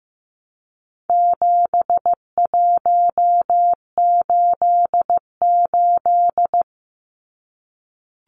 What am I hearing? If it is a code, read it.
Morse code, heard as a keyed tone, about 15 wpm, 700 Hz. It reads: 7188
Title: Numbers - 4-Digit 15wpm